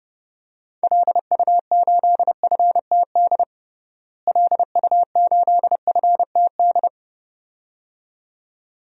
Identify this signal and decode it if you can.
Morse code, heard as a keyed tone, about 31 wpm, 700 Hz. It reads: LU8FTD LU8FTD
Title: DXCC-100 Call Signs - 6 Characters - Repeated 2x - 30wpm